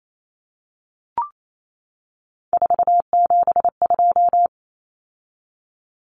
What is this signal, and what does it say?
Morse code, heard as a keyed tone, about 28 wpm, 700 Hz. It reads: 472
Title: Numbers - 3-Digit 28wpm